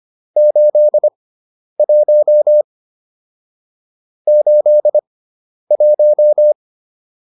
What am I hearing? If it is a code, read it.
Morse code, heard as a keyed tone, about 25 wpm, spaced as for 10 wpm, 600 Hz. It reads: 81 81
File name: QSO401 10